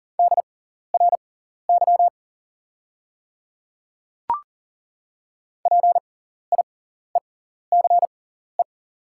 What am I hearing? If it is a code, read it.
Morse code, heard as a keyed tone, about 38 wpm, spaced as for 14 wpm, 700 Hz. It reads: DRY PIECE
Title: 6x Character Spacing - Top 500 Words - 40wpm